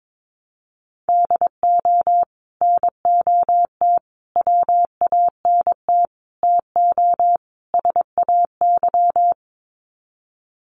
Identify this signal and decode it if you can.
Morse code, heard as a keyed tone, about 22 wpm, 700 Hz. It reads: DO NOT WANT TO SAY
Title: Sets of 5 Words Encore 22wpm